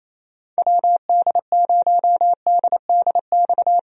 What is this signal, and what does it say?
Morse code, heard as a keyed tone, about 28 wpm, 700 Hz. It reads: WD0DDX
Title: USA Call Signs - 6 Characters - 28wpm